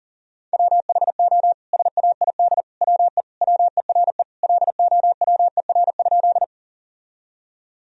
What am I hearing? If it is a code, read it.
Morse code, heard as a keyed tone, about 40 wpm, 700 Hz. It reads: WHO SAID WE WERE LOWER?
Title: Sentences from Top 1000 Words - Review Part 4 - 40wpm